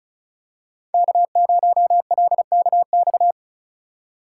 Morse code, heard as a keyed tone, about 36 wpm, 700 Hz. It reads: K0LKX
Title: USA Call Signs - 5 Characters - 4x Character Spacing - 35wpm